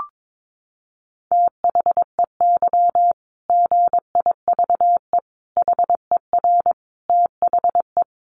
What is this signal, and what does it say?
Morse code, heard as a keyed tone, about 22 wpm, 700 Hz. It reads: THEY GIVE HER THE
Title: Easy Phrases 22wpm